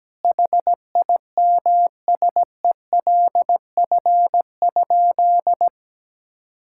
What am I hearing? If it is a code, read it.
Morse code, heard as a keyed tone, about 17 wpm, 700 Hz. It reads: HIMSELF?
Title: Sentences from Top 1600 Words 17wpm